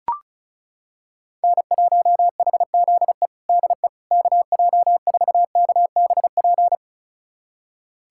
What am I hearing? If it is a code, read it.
Morse code, heard as a keyed tone, about 34 wpm, 700 Hz. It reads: N1HZE DE KJ4KBP
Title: USA Callsign DE Callsign 35wpm